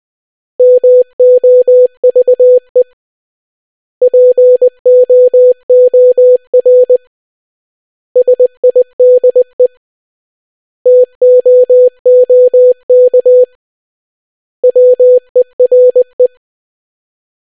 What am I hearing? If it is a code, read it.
Morse code, heard as a keyed tone, about 20 wpm, 500 Hz. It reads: MOVE POOR SIDE TOOK WERE